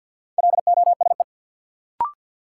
Morse code, heard as a keyed tone, about 53 wpm, 700 Hz. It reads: LOSE
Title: Sentences from Top 1100 Words - New Word Drill Alt - 50wpm